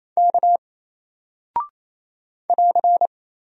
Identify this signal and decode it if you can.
Morse code, heard as a keyed tone, about 28 wpm, 700 Hz. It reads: K <AR>
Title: Single Character (Letter:Number:Punctuation:Prosign) Alt 28wpm